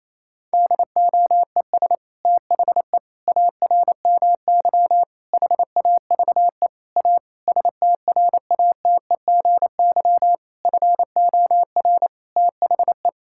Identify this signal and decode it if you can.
Morse code, heard as a keyed tone, about 28 wpm, 700 Hz. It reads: DOES THE ARMY HAVE A STRATEGY FOR THE
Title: Sentences from Top 1000 Words - Review Part 5 - 28wpm